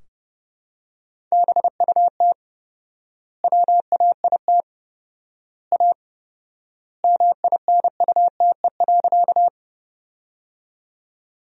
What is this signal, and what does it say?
Morse code, heard as a keyed tone, about 30 wpm, 700 Hz. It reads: BUT WAIT A MINUTE.
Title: 4x Word-Spacing - Sentences from Top 500 Words - 30wpm